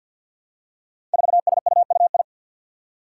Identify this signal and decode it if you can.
Morse code, heard as a keyed tone, about 47 wpm, 700 Hz. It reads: 4SUUI
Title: Five Challenging Letter-Number Combinations 50wpm